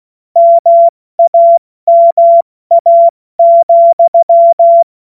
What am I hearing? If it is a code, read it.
Morse code, heard as a keyed tone, about 16 wpm, spaced as for 14 wpm, 680 Hz. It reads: MAMA,
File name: Morsekurs / Witze / Witz 16. November 2021, 70 F80